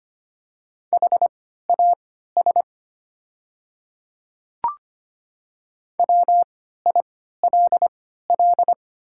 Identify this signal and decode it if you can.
Morse code, heard as a keyed tone, about 25 wpm, spaced as for 14 wpm, 700 Hz. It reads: HAS WILL
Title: 3x Character Spacing - Top 100 Words - 25wpm